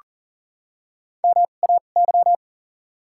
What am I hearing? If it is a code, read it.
Morse code, heard as a keyed tone, about 40 wpm, spaced as for 29 wpm, 700 Hz. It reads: MAY
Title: 2x Character Spacing - Top 200 Words - 40wpm